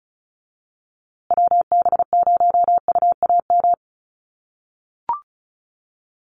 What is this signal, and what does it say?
Morse code, heard as a keyed tone, about 34 wpm, 700 Hz. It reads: WB0UAM
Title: USA Call Signs Repeated 2x 35wpm